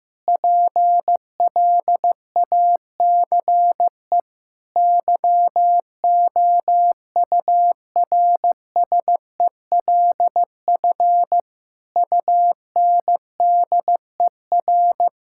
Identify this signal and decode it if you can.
Morse code, heard as a keyed tone, about 15 wpm, 700 Hz. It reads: PLACE YOURSELF UNDER
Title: Sentences from Top 1600 Words 15wpm